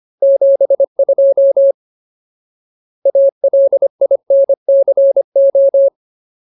Morse code, heard as a keyed tone, about 25 wpm, 550 Hz. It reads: 72 ALINCO
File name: Ultimate Word List / 25WPM Ultimate